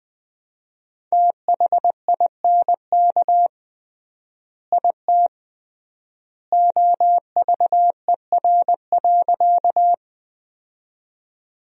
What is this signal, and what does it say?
Morse code, heard as a keyed tone, about 20 wpm, 700 Hz. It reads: THINK IT OVER.
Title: 3x Word-Spacing - Sentences from Top 200 Words - 20wpm